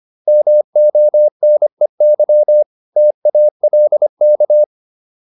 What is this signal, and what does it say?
Morse code, heard as a keyed tone, about 25 wpm, 600 Hz. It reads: MONEY TALK
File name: PR403 25